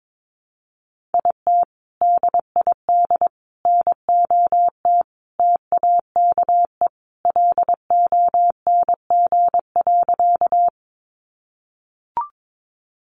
Morse code, heard as a keyed tone, about 22 wpm, 700 Hz. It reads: IT DID NOT TAKE LONG.
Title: Sentences from Top 600 Words 22wpm